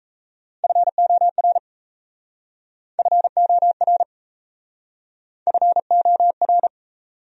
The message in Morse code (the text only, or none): FOR FOR FOR